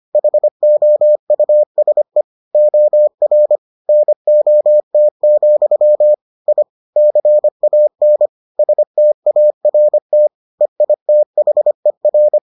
Morse code, heard as a keyed tone, about 25 wpm, 600 Hz. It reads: HOUSE OR NOT, I CAN START EITHER